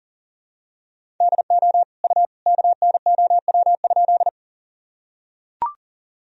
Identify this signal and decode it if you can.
Morse code, heard as a keyed tone, about 40 wpm, 700 Hz. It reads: DO U KNOW?
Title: Sentences from Top 100 Words and Abbreviations 40wpm